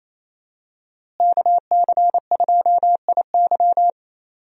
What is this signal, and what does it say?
Morse code, heard as a keyed tone, about 28 wpm, 700 Hz. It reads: KC2IY
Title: USA Call Signs - 5 Characters - Repeated 3x - 28wpm